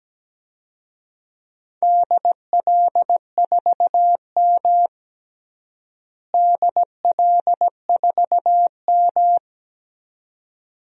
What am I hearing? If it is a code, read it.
Morse code, heard as a keyed tone, about 17 wpm, 700 Hz. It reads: DL4M DL4M
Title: DXCC-100 Call Signs - 4 Characters - Repeated 2x - 17wpm